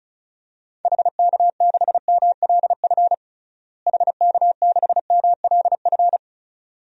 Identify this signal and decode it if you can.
Morse code, heard as a keyed tone, about 36 wpm, 700 Hz. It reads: HK6MLF HK6MLF
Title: DXCC-100 Call Signs - 6 Characters - Repeated 2x - 35wpm